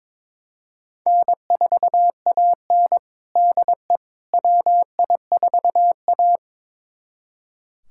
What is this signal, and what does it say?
Morse code, heard as a keyed tone, about 22 wpm, 700 Hz. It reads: N4AN DE WI4A
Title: USA Callsign DE Callsign - 4 Characters - 22wpm